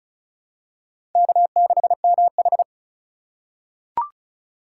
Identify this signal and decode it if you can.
Morse code, heard as a keyed tone, about 36 wpm, 700 Hz. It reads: K6MH
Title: USA Call Signs - 4 Characters - 4x Character Spacing - 35wpm